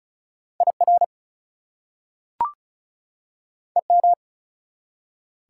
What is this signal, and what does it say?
Morse code, heard as a keyed tone, about 36 wpm, 700 Hz. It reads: IR EM
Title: Prefixes and Suffixes 35wpm